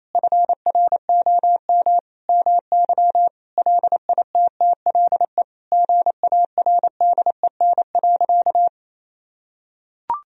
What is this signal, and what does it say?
Morse code, heard as a keyed tone, about 27 wpm, 700 Hz. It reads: FROM MY LITTLE GARDEN.
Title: Sentences from Top 1000 Words - Review Part 4 - 28wpm